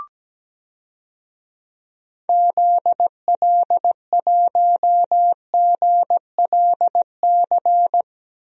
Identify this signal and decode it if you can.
Morse code, heard as a keyed tone, about 17 wpm, 700 Hz. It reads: ZL1GLC
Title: DXCC-100 Call Signs - 6 Characters - 17wpm